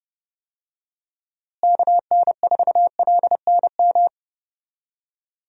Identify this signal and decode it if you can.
Morse code, heard as a keyed tone, about 30 wpm, 700 Hz. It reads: KN4LNM
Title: USA Call Signs - 6 Characters - Repeated 2x - 30wpm